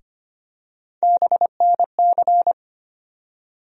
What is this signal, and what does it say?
Morse code, heard as a keyed tone, about 25 wpm, 700 Hz. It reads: BNC
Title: USA Call Sign Postfixes - 3 Letters 25wpm